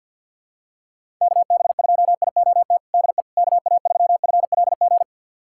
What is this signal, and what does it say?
Morse code, heard as a keyed tone, about 51 wpm, 700 Hz. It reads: KB2IOT DE KA3FLG
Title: USA Callsign DE Callsign - 6 Characters - 50wpm